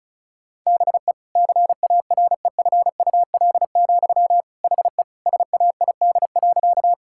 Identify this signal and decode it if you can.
Morse code, heard as a keyed tone, about 36 wpm, 700 Hz. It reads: BE CAREFUL, HE SAID.
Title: Sentences from Top 1600 Words 35wpm